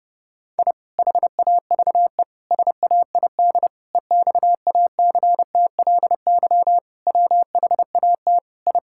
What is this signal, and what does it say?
Morse code, heard as a keyed tone, about 30 wpm, 700 Hz. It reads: I HAVE SAID EXACTLY WHAT I
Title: Sentences from Top 1000 Words - Review Part 6 - 30wpm